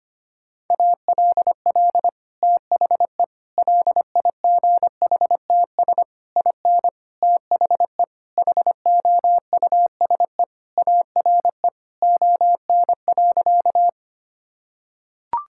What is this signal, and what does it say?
Morse code, heard as a keyed tone, about 25 wpm, 700 Hz. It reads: ALL THE LIGHTS IN THE HOUSE ARE ON.
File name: practice / 25wpm / Sentences with 2-Word Phrases - 25wpm